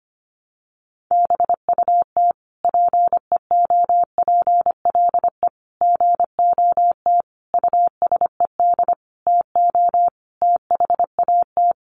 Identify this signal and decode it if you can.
Morse code, heard as a keyed tone, about 25 wpm, 700 Hz. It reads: BUT PEOPLE GOT USED TO THAT
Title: Sentences from Top 700 Words 25wpm